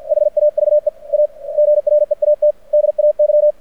Morse code, heard as a keyed tone, about 49 wpm, 600 Hz. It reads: HAVE A GREAT DAY
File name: archived / CRSnet